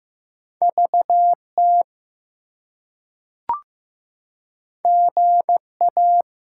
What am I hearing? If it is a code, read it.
Morse code, heard as a keyed tone, about 15 wpm, 700 Hz. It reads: VT GA
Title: US States and Canadian Provinces 15wpm